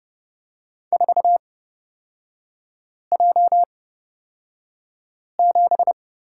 Morse code, heard as a keyed tone, about 29 wpm, spaced as for 5 wpm, 700 Hz. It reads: 4J7
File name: Lesson 36e - BT - Three Characters 30wpm